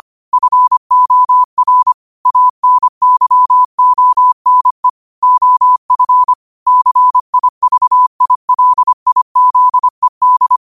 Morse code, heard as a keyed tone, about 25 wpm, 1000 Hz. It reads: FOR ANYONE OF CIVILIZED
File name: qotdfiles / 25wpm